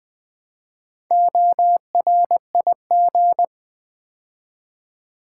Morse code, heard as a keyed tone, about 20 wpm, 700 Hz. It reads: ORIG
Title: Build A Word - 2 Letters At A Time - 8-Letter Words - 20wpm